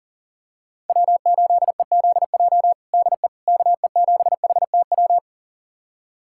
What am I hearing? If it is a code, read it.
Morse code, heard as a keyed tone, about 40 wpm, 700 Hz. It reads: W8EZJ DE KE7HTW
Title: USA Callsign DE Callsign 40wpm